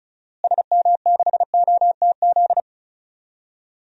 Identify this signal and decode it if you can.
Morse code, heard as a keyed tone, about 36 wpm, 700 Hz. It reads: SM6OTZ
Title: DXCC-100 Call Signs - 6 Characters - Repeated 2x - 35wpm